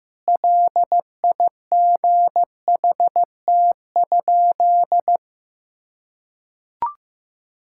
Morse code, heard as a keyed tone, about 16 wpm, 700 Hz. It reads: LIGHT?
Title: Sentences from Top 1000 Words - Review Part 5 - 15wpm